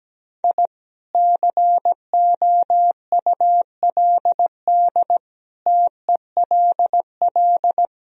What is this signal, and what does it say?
Morse code, heard as a keyed tone, about 17 wpm, 700 Hz. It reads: I COULD TELL
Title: Sentences from Top 1000 Words - Review Part 2 - 17wpm